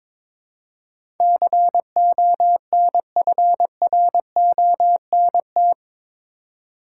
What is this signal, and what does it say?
Morse code, heard as a keyed tone, about 22 wpm, 700 Hz. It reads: CONFRONT
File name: practice / 22wpm / Eight Letter Words - 22wpm